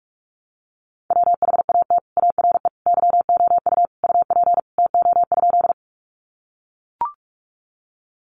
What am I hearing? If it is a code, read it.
Morse code, heard as a keyed tone, about 44 wpm, 700 Hz. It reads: WHAT ARE YOU UP TO?
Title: Sentences from Top 100 Words 45wpm